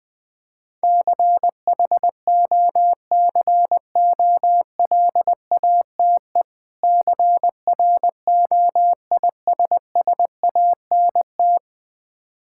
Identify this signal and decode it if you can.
Morse code, heard as a keyed tone, about 20 wpm, 700 Hz. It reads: CHOCOLATE CROISSANT
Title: Difficult Words with Hints - 20wpm